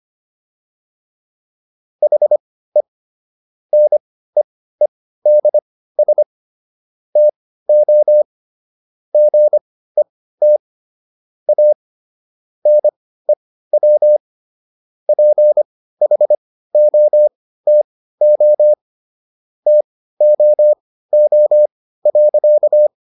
Morse code, heard as a keyed tone, about 25 wpm, spaced as for 15 wpm, 600 Hz. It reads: HE NEEDS TO GET A NEW PHOTO TOO.